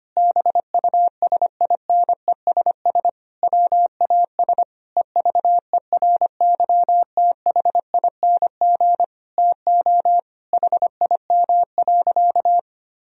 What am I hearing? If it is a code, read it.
Morse code, heard as a keyed tone, about 25 wpm, 700 Hz. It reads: BUSINESS WAS EVERYTHING TO HIM.
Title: Sentences from Top 1000 Words - Review Part 3 - 25wpm